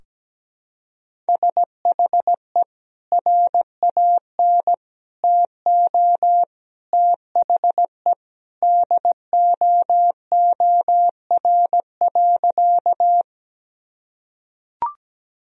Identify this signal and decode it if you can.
Morse code, heard as a keyed tone, about 17 wpm, 700 Hz. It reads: SHE RAN TO THE DOOR.
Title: Sentences from Top 1000 Words - Review Part 3 - 17wpm